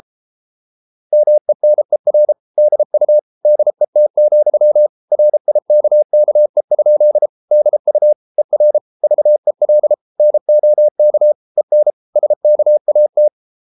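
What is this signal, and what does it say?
Morse code, heard as a keyed tone, about 33 wpm, 600 Hz. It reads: MENER DU DET, RIKKE? DU ER VEL NOK EN SKAT
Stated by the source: Morse code at 33 wpm, QRQ.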